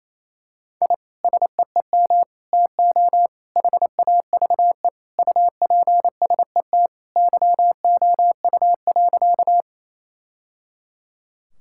Morse code, heard as a keyed tone, about 27 wpm, 700 Hz. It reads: I SEEM TO HAVE UPSET YOU.
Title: Sentences from Top 1000 Words - Review Part 5 - 28wpm